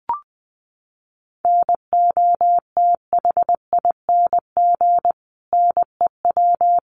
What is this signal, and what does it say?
Morse code, heard as a keyed tone, about 20 wpm, 700 Hz. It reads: NOTHING NEW
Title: Sentences from Top 500 Words 20wpm